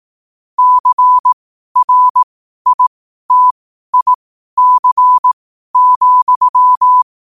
Morse code, heard as a keyed tone, about 18 wpm, spaced as for 12 wpm, 1000 Hz. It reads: CRITIC,
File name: qotdfiles / 12wpm